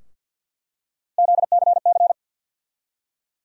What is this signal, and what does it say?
Morse code, heard as a keyed tone, about 50 wpm, 700 Hz. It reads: ZCC
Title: USA Call Sign Postfixes - 3 Letters 50wpm